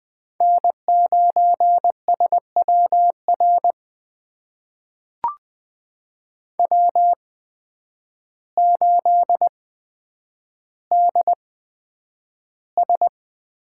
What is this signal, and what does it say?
Morse code, heard as a keyed tone, about 20 wpm, spaced as for 9 wpm, 700 Hz. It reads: N9SWR W8DS
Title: USA Call Signs - 5 Characters - 8x Character Spacing - 20wpm